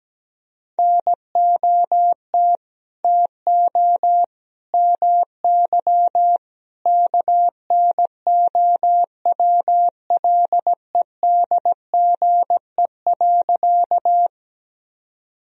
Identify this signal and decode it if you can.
Morse code, heard as a keyed tone, about 17 wpm, 700 Hz. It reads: NOT TO MY KNOWLEDGE.
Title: Sentences from Top 1000 Words - Review Part 3 - 17wpm